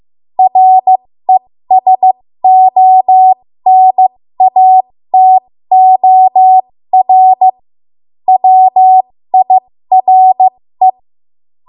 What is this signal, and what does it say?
Morse code, heard as a keyed tone, about 15 wpm, spaced as for 13 wpm, 750 Hz. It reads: RESONATOR WIRE
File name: Morse / 13WPM